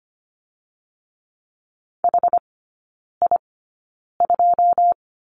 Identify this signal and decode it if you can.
Morse code, heard as a keyed tone, about 25 wpm, spaced as for 9 wpm, 690 Hz. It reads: HI2
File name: Lesson 14e - Number 2 - Three Characters 25wpm